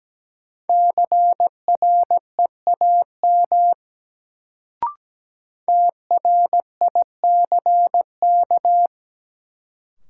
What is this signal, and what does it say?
Morse code, heard as a keyed tone, about 17 wpm, 700 Hz. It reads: CREAM TRICK
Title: Sentences from Top 1500 Words - New Word Drill - 17wpm